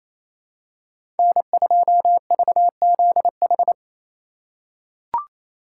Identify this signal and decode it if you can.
Morse code, heard as a keyed tone, about 28 wpm, 700 Hz. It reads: N2VZH
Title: USA Call Signs - 5 Characters - 2x Character Spacing - 28wpm